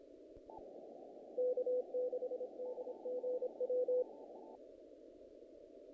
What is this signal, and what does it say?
Morse code, heard as a keyed tone, about 26 wpm, 500 Hz. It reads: K6DGW
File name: contests / naqp-cw-k5tr-op-k5pi- -1800